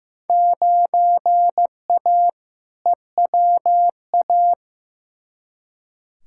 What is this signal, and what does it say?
Morse code, heard as a keyed tone, about 15 wpm, 700 Hz. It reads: 9A EWA
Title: Field Day Exchanges Encore - 15wpm